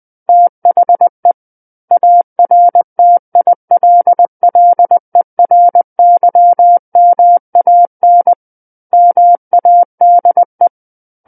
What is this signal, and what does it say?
Morse code, heard as a keyed tone, about 20 wpm, 700 Hz. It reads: THE ARTILLERYMAN MADE
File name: 20 WPM